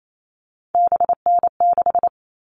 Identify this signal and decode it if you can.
Morse code, heard as a keyed tone, about 28 wpm, 700 Hz. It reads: BN6
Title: All Cut Numbers - 3-Digit 28wpm